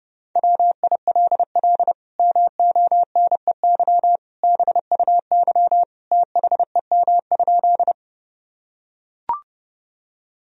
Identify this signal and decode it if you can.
Morse code, heard as a keyed tone, about 30 wpm, 700 Hz. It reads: WILL MONEY BUY THEM?
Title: Sentences from Top 1000 Words - Review Part 4 - 30wpm